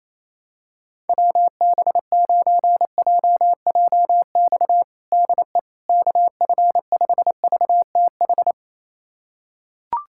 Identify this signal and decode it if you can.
Morse code, heard as a keyed tone, about 28 wpm, 700 Hz. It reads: WB9JJX DE KF5VTH
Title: USA Callsign DE Callsign - 6 Characters - 28wpm